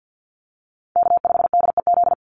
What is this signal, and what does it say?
Morse code, heard as a keyed tone, about 48 wpm, 700 Hz. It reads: K5DEZ
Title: NA SOTA Chaser Callsigns 50wpm